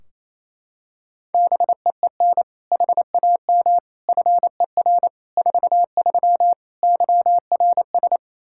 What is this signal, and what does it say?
Morse code, heard as a keyed tone, about 28 wpm, 700 Hz. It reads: BEEN HAM FER 43 YRS
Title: Practice QSO - Exchange 3 - More Info - 28wpm